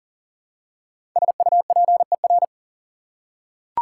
Morse code, heard as a keyed tone, about 40 wpm, 700 Hz. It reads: SUPER
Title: Prefixes and Suffixes 40wpm